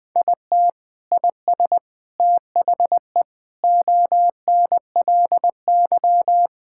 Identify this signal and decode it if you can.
Morse code, heard as a keyed tone, about 20 wpm, 700 Hz. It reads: IT IS THE ONLY
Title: Sentences from Top 1000 Words - Review Part 5 - 20wpm